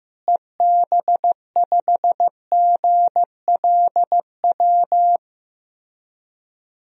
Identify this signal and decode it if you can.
Morse code, heard as a keyed tone, about 15 wpm, 700 Hz. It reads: EB5GLW
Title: DXCC-100 Call Signs - 6 Characters - Repeated 3x - 15wpm